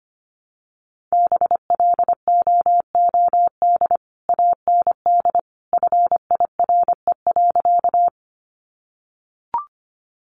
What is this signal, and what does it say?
Morse code, heard as a keyed tone, about 25 wpm, 700 Hz. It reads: BLOOD AND FIRE.
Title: Sentences from Top 1100 Words 25wpm